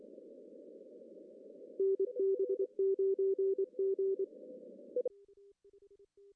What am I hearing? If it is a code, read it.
Morse code, heard as a keyed tone, about 25 wpm, 380 Hz. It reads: NB9G
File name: CW / cqww-cw-k5tr